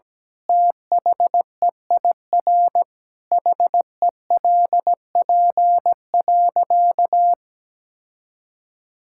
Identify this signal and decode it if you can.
Morse code, heard as a keyed tone, about 17 wpm, 700 Hz. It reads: THEIR HELP.
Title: Sentences from Top 1000 Words - Review Part 6 - 17wpm